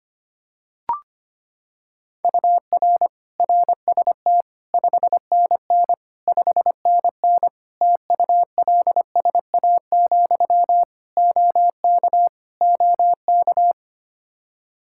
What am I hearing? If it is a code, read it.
Morse code, heard as a keyed tone, about 25 wpm, 700 Hz. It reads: UR RST 5NN 5NN TULSA, OK OK
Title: Practice QSO - Exchange 1 - Beginning - 25wpm